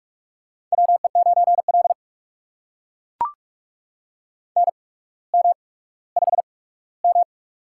Morse code, heard as a keyed tone, about 44 wpm, spaced as for 19 wpm, 690 Hz. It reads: WE9L NM5M
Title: USA Call Signs - 4 Characters - 8x Character Spacing - 45wpm